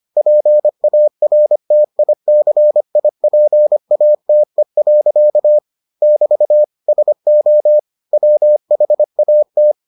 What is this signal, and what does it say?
Morse code, heard as a keyed tone, about 25 wpm, 600 Hz. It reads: PARTICIPATE. <BT> SO WHAT